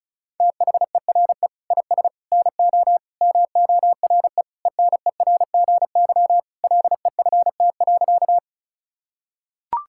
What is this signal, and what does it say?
Morse code, heard as a keyed tone, about 34 wpm, 700 Hz. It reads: THERE IS NO MORE ENERGY LEFT.
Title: Sentences from Top 1000 Words - Review Part 6 - 35wpm